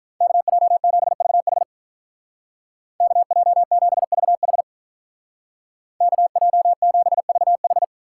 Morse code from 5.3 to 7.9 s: KJ7VH

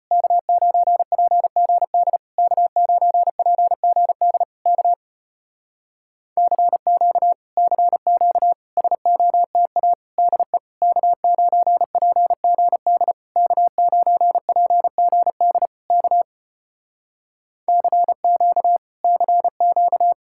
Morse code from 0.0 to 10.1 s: K9PGD K9PGD K CQ CQ SOTA

